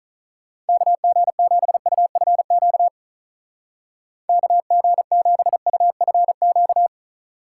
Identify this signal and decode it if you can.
Morse code, heard as a keyed tone, about 44 wpm, 700 Hz. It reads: KG7UFQ KG7UFQ
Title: Speed Racing - USA 2x3 Call Signs 35wpm